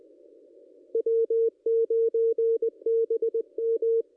Morse code, heard as a keyed tone, about 20 wpm, 460 Hz. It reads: W9BM
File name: contests / ss-cw-k5tr-op-n5rz -2200